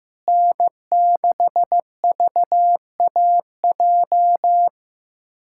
Morse code, heard as a keyed tone, about 15 wpm, 700 Hz. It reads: N6VAJ
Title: USA Callsign DE Callsign - 5 Characters - 15wpm